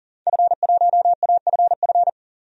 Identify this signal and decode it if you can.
Morse code, heard as a keyed tone, about 40 wpm, 700 Hz. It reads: F1AFF
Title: DXCC-100 Call Signs - 5 Characters - 40wpm